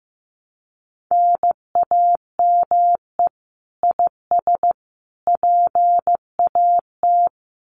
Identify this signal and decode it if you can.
Morse code, heard as a keyed tone, about 15 wpm, 700 Hz. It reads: NAME IS PAT
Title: Common QSO Phrases 15wpm